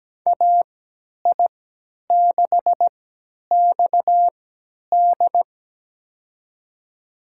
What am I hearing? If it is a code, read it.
Morse code, heard as a keyed tone, about 17 wpm, spaced as for 10 wpm, 700 Hz. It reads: AI6XD
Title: USA Call Signs - 5 Characters - 3x Character Spacing - 17wpm